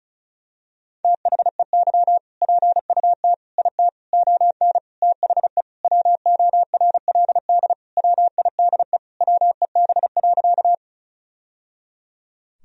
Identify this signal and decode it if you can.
Morse code, heard as a keyed tone, about 36 wpm, 700 Hz. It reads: THEY PUT IT ON THE WORLD WIDE WEB.
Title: Sentences from Top 1000 Words - Review Part 4 - 35wpm